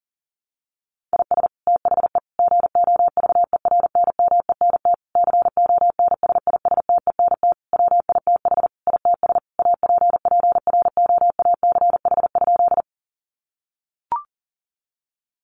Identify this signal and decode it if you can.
Morse code, heard as a keyed tone, about 40 wpm, 700 Hz. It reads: IS THE GOVERNMENT CONSISTENT WITH ITS APPROACH?
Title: Sentences from Top 1500 Words 40wpm